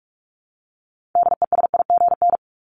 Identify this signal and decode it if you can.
Morse code, heard as a keyed tone, about 44 wpm, 700 Hz. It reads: DESIGN
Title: ED Suffix - Top 100 Words with Base Words 45wpm